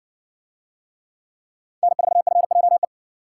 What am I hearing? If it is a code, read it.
Morse code, heard as a keyed tone, about 57 wpm, 700 Hz. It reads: N4FJE